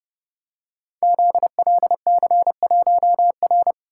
Morse code, heard as a keyed tone, about 30 wpm, 700 Hz. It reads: ZLC1R
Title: Code Groups - 5 Letter:Numbers - 30wpm